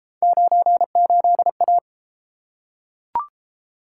Morse code, heard as a keyed tone, about 31 wpm, 700 Hz. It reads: 98A